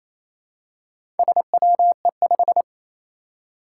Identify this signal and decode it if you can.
Morse code, heard as a keyed tone, about 28 wpm, 700 Hz. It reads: SWE5
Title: Four Challenging Letter-Number Combinations 28wpm